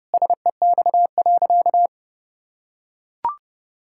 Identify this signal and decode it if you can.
Morse code, heard as a keyed tone, about 30 wpm, 700 Hz. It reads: SEX.
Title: Sentences from Top 1600 Words 30wpm